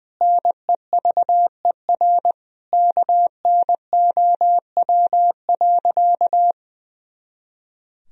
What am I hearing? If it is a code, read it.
Morse code, heard as a keyed tone, about 21 wpm, 700 Hz. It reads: NEVER KNOW.
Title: Sentences from Top 300 Words and Abbreviations 20wpm